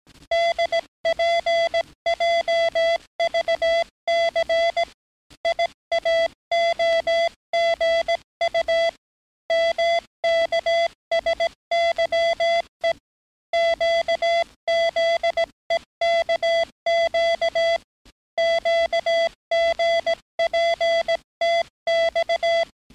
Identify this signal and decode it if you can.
Morse code, heard as a keyed tone, about 18 wpm, 670 Hz. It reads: DPJVC IAOGU MKSYE QZEKQ QGPTX
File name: training / NAVPERS 11049to11060 / NAVPERS 11058 RA RB